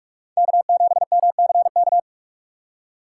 Morse code, heard as a keyed tone, about 46 wpm, 700 Hz. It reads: K7MCK
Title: USA Call Signs - 5 Characters - 3x Character Spacing - 45wpm